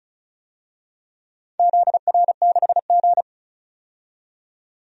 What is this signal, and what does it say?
Morse code, heard as a keyed tone, about 36 wpm, 700 Hz. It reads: ZR6G
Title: DXCC-100 Call Signs - 4 Characters - 35wpm